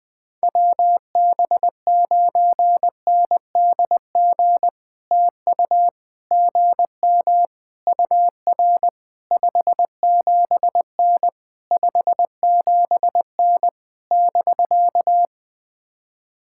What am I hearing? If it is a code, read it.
Morse code, heard as a keyed tone, about 19 wpm, 700 Hz. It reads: WB9NDG TU GM UR 57N 57N <BK>
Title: POTA QSO Practice - 20wpm